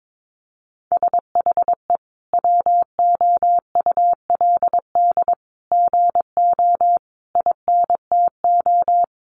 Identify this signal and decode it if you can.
Morse code, heard as a keyed tone, about 22 wpm, 700 Hz. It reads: SHE WOULD GO INTO
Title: Sentences from Top 1300 Words 22wpm